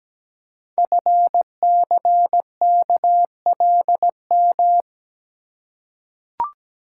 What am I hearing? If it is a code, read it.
Morse code, heard as a keyed tone, about 17 wpm, 700 Hz. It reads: FCKLM
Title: Code Groups - 5 Letters - 17wpm